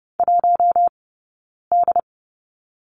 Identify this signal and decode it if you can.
Morse code, heard as a keyed tone, about 30 wpm, 700 Hz. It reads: 1 D
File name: course / 30:10wpm / 086 Lesson 15d - Number 5 - Two Characters 30wpm